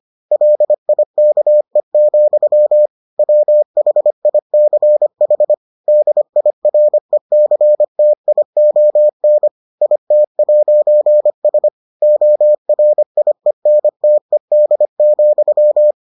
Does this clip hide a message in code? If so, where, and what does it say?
Yes: Morse code from 0.3 to 15.9 s, heard as a keyed tone, about 25 wpm, 600 Hz. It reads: LIKE, WHICH DIRECTION IT'S ORIENTED,